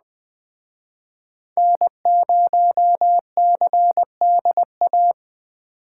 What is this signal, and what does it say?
Morse code, heard as a keyed tone, about 21 wpm, 700 Hz. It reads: N0CDA